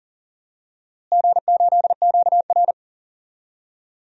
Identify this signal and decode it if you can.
Morse code, heard as a keyed tone, about 37 wpm, 700 Hz. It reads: G8QR
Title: DXCC-100 Call Signs - 4 Characters - Repeated 3x - 40wpm